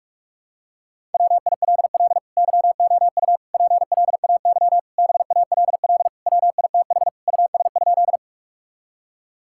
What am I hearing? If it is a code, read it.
Morse code, heard as a keyed tone, about 46 wpm, 700 Hz. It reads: WILL YOU PLAY BALL WITH US?
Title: Sentences from Top 1000 Words - Review Part 6 - 45wpm